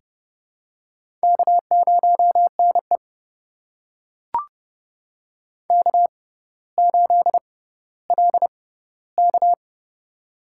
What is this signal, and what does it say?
Morse code, heard as a keyed tone, about 30 wpm, spaced as for 15 wpm, 700 Hz. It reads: K0NE K8LK
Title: USA Call Signs - 4 Characters - 6x Character Spacing - 30wpm